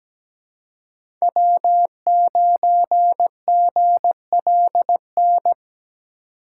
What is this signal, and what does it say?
Morse code, heard as a keyed tone, about 17 wpm, 700 Hz. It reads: W9GLN
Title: USA Call Signs - 5 Characters - 3x Character Spacing - 17wpm